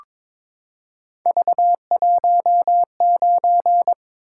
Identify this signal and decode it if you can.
Morse code, heard as a keyed tone, about 22 wpm, 700 Hz. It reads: V19
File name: practice / 22wpm / Three Letter-Number Combinations - 22wpm